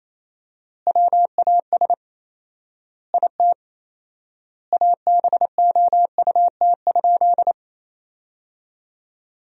Morse code, heard as a keyed tone, about 28 wpm, 700 Hz. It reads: WAS IT ABOUT?
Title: 4x Word-Spacing - Sentences from Top 100 Words - 28wpm